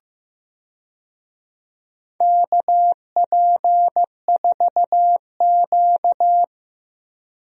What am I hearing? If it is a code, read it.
Morse code, heard as a keyed tone, about 15 wpm, 700 Hz. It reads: KP4Q
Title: DXCC-100 Call Signs - 4 Characters - 15wpm